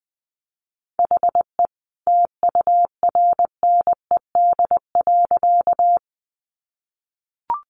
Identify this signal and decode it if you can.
Morse code, heard as a keyed tone, about 20 wpm, 700 Hz. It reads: HE TURNED.
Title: A Sound Of Thunder 20wpm